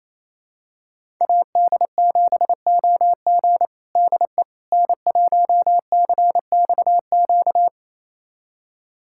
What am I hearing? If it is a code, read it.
Morse code, heard as a keyed tone, about 27 wpm, 700 Hz. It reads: AD7OG DE N1CXQ
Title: USA Callsign DE Callsign - 5 Characters - 28wpm